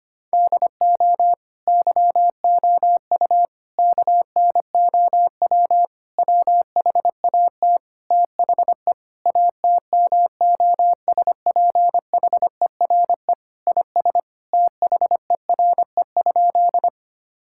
Morse code, heard as a keyed tone, about 25 wpm, 700 Hz. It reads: DO YOU KNOW WHAT THE ATMOSPHERE IS THERE?
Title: Sentences from Top 1600 Words 25wpm